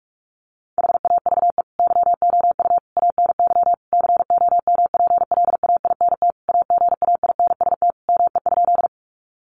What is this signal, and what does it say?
Morse code, heard as a keyed tone, about 44 wpm, 700 Hz. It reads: HAVE YOU ANY COMPLAINT AGAINST ME?
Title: Sentences from Top 1400 Words 45wpm